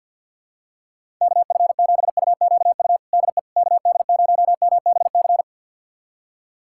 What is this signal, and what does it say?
Morse code, heard as a keyed tone, about 52 wpm, 700 Hz. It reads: KF7UQU DE KD0MBC
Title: USA Callsign DE Callsign - 6 Characters - 50wpm